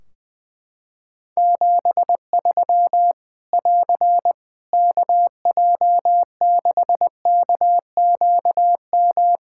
Morse code, heard as a keyed tone, about 20 wpm, 700 Hz. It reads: 73 <AR> KJ6KQM
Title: Practice QSO - Exchange 4 - Ending - 20wpm